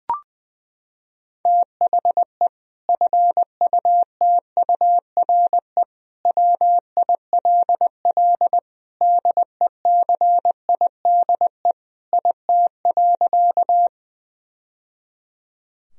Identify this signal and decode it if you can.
Morse code, heard as a keyed tone, about 20 wpm, 700 Hz. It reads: THE FUTURE WILL DECIDE IT.
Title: Sentences from Top 1000 Words - Review Part 3 - 20wpm